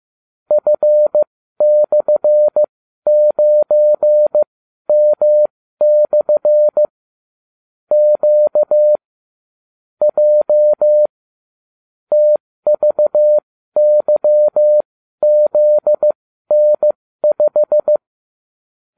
Morse code, heard as a keyed tone, about 15 wpm, spaced as for 12 wpm, 600 Hz. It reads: F/9M/ Q J TVYZN5